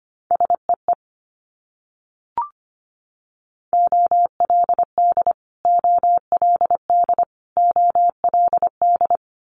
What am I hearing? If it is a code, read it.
Morse code, heard as a keyed tone, about 25 wpm, 700 Hz. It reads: SEE OLD OLD OLD
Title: Multi-letter Phonemes Repeated 3x 25wpm